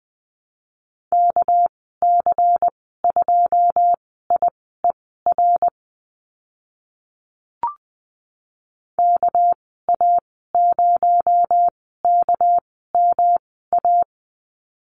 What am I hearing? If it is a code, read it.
Morse code, heard as a keyed tone, about 20 wpm, spaced as for 14 wpm, 700 Hz. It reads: KC2IER KA0KMA
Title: 245 Lesson 39g - Letter z - Call Signs 20wpm